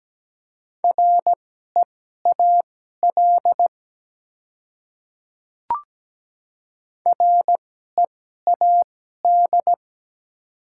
Morse code, heard as a keyed tone, about 17 wpm, spaced as for 12 wpm, 700 Hz. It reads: REAL READ
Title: 2x Character Spacing - Top 300 Words - 17wpm